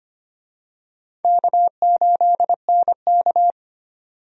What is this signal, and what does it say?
Morse code, heard as a keyed tone, about 25 wpm, 700 Hz. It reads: K8NK
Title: USA Call Signs - 4 Characters - 25wpm